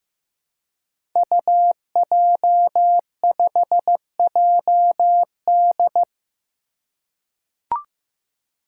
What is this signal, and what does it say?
Morse code, heard as a keyed tone, about 15 wpm, 700 Hz. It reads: UJ5JD
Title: Code Groups - 5 Letter:Numbers - 15wpm